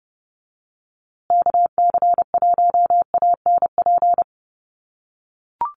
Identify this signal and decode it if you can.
Morse code, heard as a keyed tone, about 30 wpm, 700 Hz. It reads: KC1ANP
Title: USA Call Signs Repeated 2x 30wpm